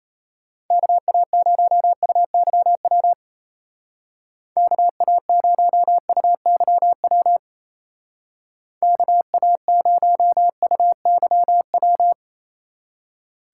Morse code, heard as a keyed tone, about 33 wpm, 700 Hz. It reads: KA0UYW KA0UYW KA0UYW
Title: Speed Racing - USA 2x3 Call Signs 28wpm